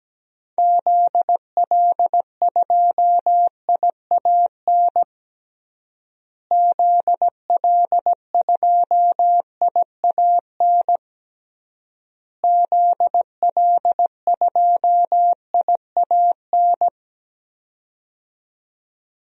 Morse code, heard as a keyed tone, about 17 wpm, 700 Hz. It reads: ZL2IAN ZL2IAN ZL2IAN
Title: DXCC-100 Call Signs - 6 Characters - Repeated 3x - 17wpm